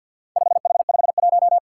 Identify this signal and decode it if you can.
Morse code, heard as a keyed tone, about 51 wpm, 700 Hz. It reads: 5H51
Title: Four Challenging Letter-Number Combinations 50wpm